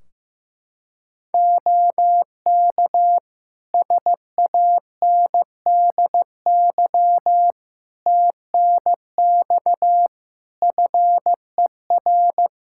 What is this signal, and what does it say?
Morse code, heard as a keyed tone, about 15 wpm, 700 Hz. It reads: OK SANDY TNX FER
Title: K1USN SST QSO Practice 15wpm